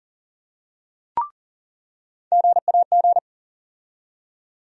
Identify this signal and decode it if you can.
Morse code, heard as a keyed tone, about 40 wpm, 700 Hz. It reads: GAG
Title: USA Call Sign Postfixes - 3 Letters 40wpm